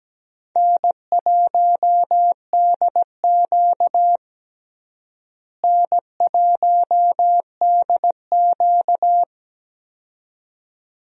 Morse code, heard as a keyed tone, about 17 wpm, 700 Hz. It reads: N1DQ N1DQ
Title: USA Call Signs - 4 Characters - Repeated 2x - 17wpm